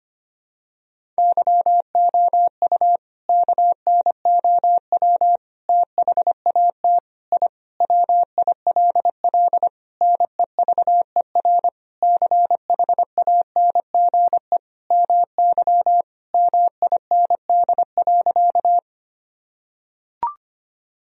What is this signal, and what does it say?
Morse code, heard as a keyed tone, about 25 wpm, 700 Hz. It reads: YOU KNOW THAT I WILL NEVER CHANGE MY MIND.
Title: Sentences from Top 1000 Words - Review Part 1 - 25wpm